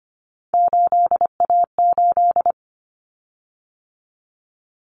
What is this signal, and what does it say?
Morse code, heard as a keyed tone, about 25 wpm, 700 Hz. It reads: 8A8
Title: Common Cut Numbers - 3-Digit 25wpm